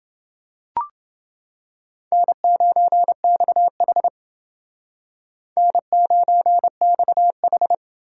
Morse code, heard as a keyed tone, about 30 wpm, 700 Hz. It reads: N9XH N9XH